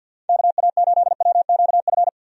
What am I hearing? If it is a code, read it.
Morse code, heard as a keyed tone, about 48 wpm, 700 Hz. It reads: KA8WQF
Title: USA Call Signs - 6 Characters - 50wpm